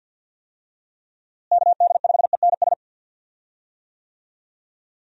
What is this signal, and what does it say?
Morse code, heard as a keyed tone, about 46 wpm, 700 Hz. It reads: KD5ENS
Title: USA Call Signs - 6 Characters - Repeated 2x - 50wpm